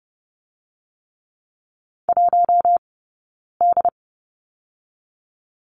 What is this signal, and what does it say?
Morse code, heard as a keyed tone, about 30 wpm, spaced as for 9 wpm, 700 Hz. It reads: 1D
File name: Lesson 18d - Letter m - Two Characters 30wpm